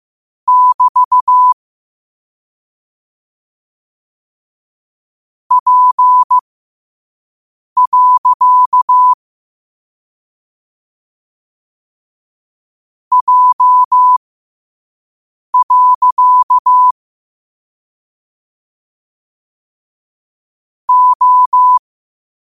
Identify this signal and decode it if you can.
Morse code, heard as a keyed tone, about 15 wpm, spaced as for 5 wpm, 1000 Hz. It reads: <BT> P. J. O
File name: qotdfiles / 5wpm